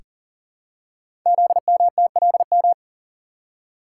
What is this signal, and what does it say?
Morse code, heard as a keyed tone, about 39 wpm, 690 Hz. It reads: ZMTLM
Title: Code Groups - 5 Letters - 40wpm